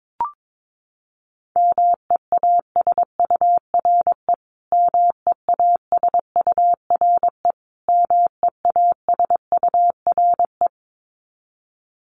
Morse code, heard as a keyed tone, about 22 wpm, 700 Hz. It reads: MEASURE MEASURE MEASURE
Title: Top 500 Words Repeated 22wpm